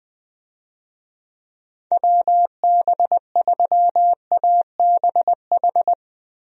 Morse code, heard as a keyed tone, about 21 wpm, 700 Hz. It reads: WB3ABH
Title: USA Call Signs - 6 Characters - 20wpm